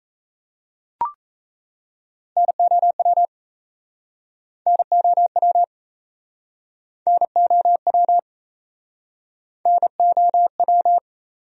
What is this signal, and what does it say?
Morse code, heard as a keyed tone, about 37 wpm, 700 Hz. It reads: NOW NOW NOW NOW